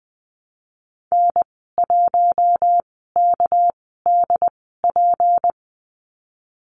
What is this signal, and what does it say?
Morse code, heard as a keyed tone, about 20 wpm, spaced as for 15 wpm, 700 Hz. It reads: N1KDP
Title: 213 Lesson 34g - Number 8 - Call Signs 20wpm